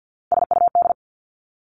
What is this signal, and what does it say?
Morse code, heard as a keyed tone, about 50 wpm, 700 Hz. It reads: SUD
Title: Three Challenging Letter-Number Combinations 50wpm